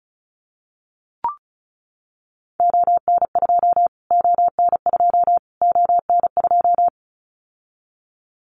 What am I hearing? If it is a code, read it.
Morse code, heard as a keyed tone, about 34 wpm, 700 Hz. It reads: ON2 ON2 ON2
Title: Callsign Prefixes for Top 25 DXCC - Repeated 3x - 35wpm